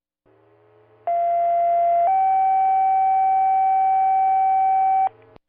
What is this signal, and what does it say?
Two-tone page, heard as a one-second tone, then a three-second tone, about 690 then 770 Hz. It is heard from 1.1 to 5.1 s.
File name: Reading-Fire-Tone